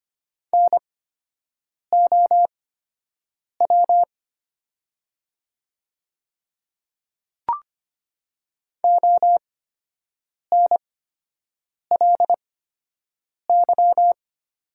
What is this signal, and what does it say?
Morse code, heard as a keyed tone, about 25 wpm, spaced as for 7 wpm, 700 Hz. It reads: NOW ONLY
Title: 8x Character Spacing - Top 300 Words - 25wpm